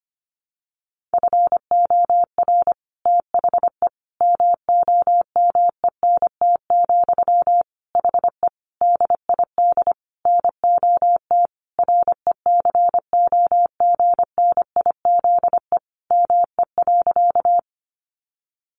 Morse code, heard as a keyed tone, about 25 wpm, 700 Hz. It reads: FOR THE MOMENT, HE DID NOT RECOGNIZE ME.
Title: Sentences from Top 1400 Words 25wpm